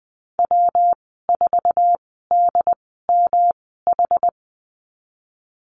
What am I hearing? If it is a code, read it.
Morse code, heard as a keyed tone, about 20 wpm, spaced as for 14 wpm, 700 Hz. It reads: W4DMH
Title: 140 Lesson 23g - Letter f - Call Signs 20wpm